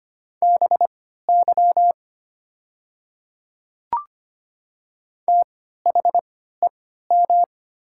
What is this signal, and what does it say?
Morse code, heard as a keyed tone, about 25 wpm, spaced as for 14 wpm, 700 Hz. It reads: BY THEM
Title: 3x Character Spacing - Top 100 Words - 25wpm